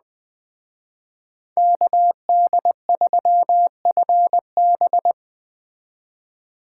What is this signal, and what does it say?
Morse code, heard as a keyed tone, about 20 wpm, 700 Hz. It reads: KD3FB
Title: Speed Racing - USA 2x2 Call Signs 17wpm